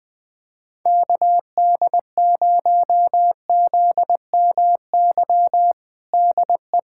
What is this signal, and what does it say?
Morse code, heard as a keyed tone, about 20 wpm, 700 Hz. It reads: KD0ZMY DE
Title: USA Callsign DE Callsign - 6 Characters - 20wpm